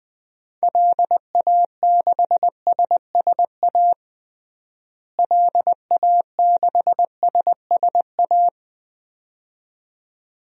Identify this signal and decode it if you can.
Morse code, heard as a keyed tone, about 20 wpm, 700 Hz. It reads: LA6SSA LA6SSA
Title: DXCC-100 Call Signs - 6 Characters - Repeated 2x - 20wpm